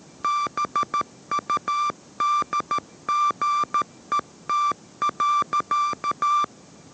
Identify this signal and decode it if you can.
Morse code, heard as a keyed tone, about 13 wpm, 1180 Hz. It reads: BUDGET.